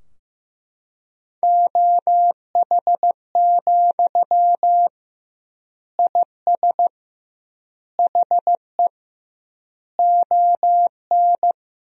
Morse code, heard as a keyed tone, about 16 wpm, 700 Hz. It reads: OH, IS HE ON
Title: 2x Word-Spacing - Sentences from Top 500 Words - 15wpm